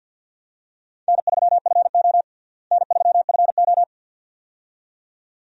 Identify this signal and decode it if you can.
Morse code, heard as a keyed tone, about 51 wpm, 700 Hz. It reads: N3FO N3FO
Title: USA Call Signs - 4 Characters - Repeated 2x - 50wpm